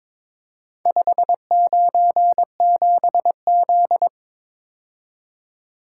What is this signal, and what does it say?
Morse code, heard as a keyed tone, about 22 wpm, 700 Hz. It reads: 597Z
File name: practice / 22wpm / Four Letter-Number Combinations - 22wpm